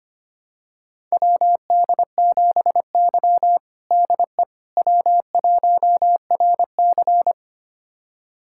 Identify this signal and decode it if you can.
Morse code, heard as a keyed tone, about 25 wpm, 700 Hz. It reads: WD7Y DE W1RC
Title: USA Callsign DE Callsign - 4 Characters - 25wpm